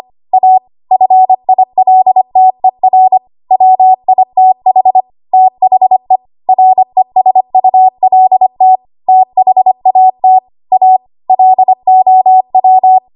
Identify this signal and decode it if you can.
Morse code, heard as a keyed tone, about 25 wpm, 750 Hz. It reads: A FILTER WITH THE RESULT THAT A LOW
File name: Archive / 25 WPM / 25WPM